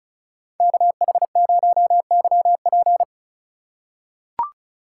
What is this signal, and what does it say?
Morse code, heard as a keyed tone, about 36 wpm, 700 Hz. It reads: KH0YP
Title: USA Call Signs - 5 Characters - 3x Character Spacing - 35wpm